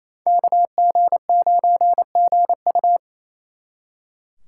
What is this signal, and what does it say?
Morse code, heard as a keyed tone, about 36 wpm, 700 Hz. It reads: KG9GU